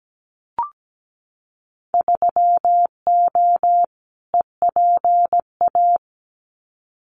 Morse code, heard as a keyed tone, about 17 wpm, 700 Hz. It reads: 3O EPA
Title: Winter Field Day Exchanges 17wpm